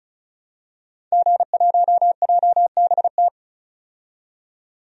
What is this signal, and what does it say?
Morse code, heard as a keyed tone, about 34 wpm, 700 Hz. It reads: G1JBT
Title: DXCC-100 Call Signs - 5 Characters - 35wpm